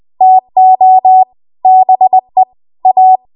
Morse code, heard as a keyed tone, about 20 wpm, 750 Hz. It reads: TO BE A
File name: Archive / 20 WPM / 20WPM